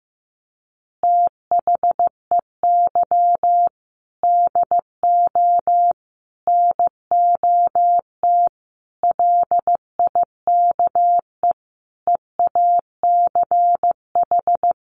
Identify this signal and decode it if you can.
Morse code, heard as a keyed tone, about 15 wpm, 700 Hz. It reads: THEY DO NOT LIKE EACH
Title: Easy Phrases 15wpm